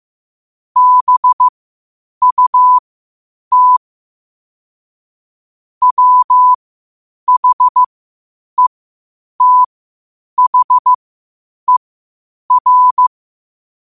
Morse code, heard as a keyed tone, about 15 wpm, spaced as for 8 wpm, 1000 Hz. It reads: BUT WHETHER
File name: qotdfiles / 8wpm